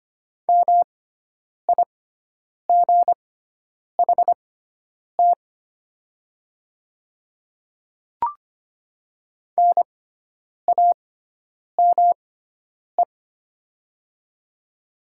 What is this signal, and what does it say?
Morse code, heard as a keyed tone, about 25 wpm, spaced as for 9 wpm, 700 Hz. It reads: MIGHT NAME
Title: 6x Character Spacing - Top 300 Words - 25wpm